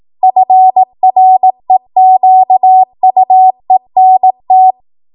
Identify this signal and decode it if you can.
Morse code, heard as a keyed tone, about 18 wpm, 750 Hz. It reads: FREQUENT
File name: Morse / 18WPM